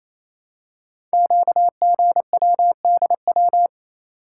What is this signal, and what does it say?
Morse code, heard as a keyed tone, about 29 wpm, 700 Hz. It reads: QGWDW
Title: Code Groups - 5 Letters - 28wpm